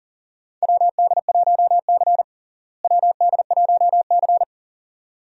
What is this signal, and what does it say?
Morse code, heard as a keyed tone, about 40 wpm, 700 Hz. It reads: WD1C WD1C
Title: USA Call Signs - 4 Characters - Repeated 2x - 40wpm